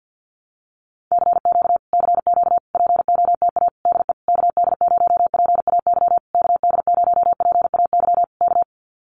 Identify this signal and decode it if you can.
Morse code, heard as a keyed tone, about 50 wpm, 700 Hz. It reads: CQ CQ POTA DE KD0PAY KD0PAY K
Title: Common QSO Phrases Encore 50wpm